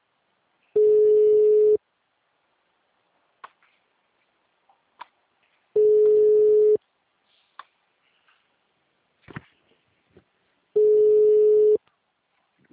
Ringback tone (Europe, 425 Hz).